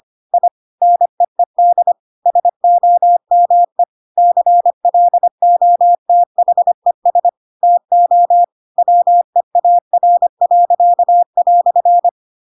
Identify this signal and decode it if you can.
Morse code, heard as a keyed tone, about 25 wpm, 700 Hz. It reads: I NEED SOME CLOTHES TO WEAR."